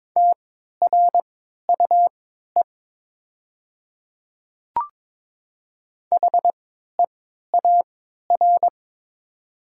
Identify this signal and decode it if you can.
Morse code, heard as a keyed tone, about 22 wpm, spaced as for 12 wpm, 700 Hz. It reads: TRUE HEAR
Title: 3x Character Spacing - Top 500 Words - 22wpm